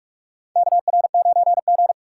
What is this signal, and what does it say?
Morse code, heard as a keyed tone, about 43 wpm, 700 Hz. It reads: KR9G
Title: USA Call Signs - 4 Characters - 4x Character Spacing - 45wpm